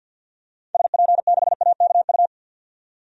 Morse code, heard as a keyed tone, about 47 wpm, 700 Hz. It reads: SP6AKU
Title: DXCC-100 Call Signs - 6 Characters - 50wpm